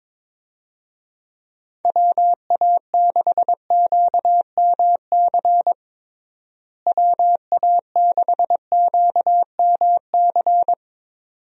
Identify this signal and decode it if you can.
Morse code, heard as a keyed tone, about 22 wpm, 700 Hz. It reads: WA6QMC WA6QMC
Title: USA Call Signs - 6 Characters - Repeated 2x - 22wpm